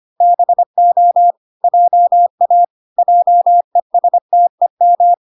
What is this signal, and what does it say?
Morse code, heard as a keyed tone, about 25 wpm, 700 Hz. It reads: BO JA JESTEM